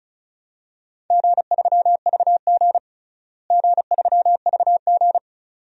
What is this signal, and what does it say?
Morse code, heard as a keyed tone, about 34 wpm, 700 Hz. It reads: G3VG G3VG
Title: DXCC-100 Call Signs - 4 Characters - Repeated 2x - 35wpm